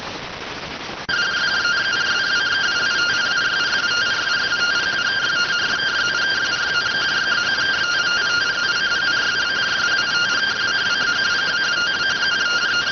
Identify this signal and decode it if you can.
Two-tone teleprinter signal, about 170 Hz shift.